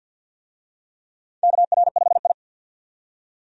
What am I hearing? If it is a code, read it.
Morse code, heard as a keyed tone, about 46 wpm, 700 Hz. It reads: KR5I
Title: USA Call Signs - 4 Characters - 50wpm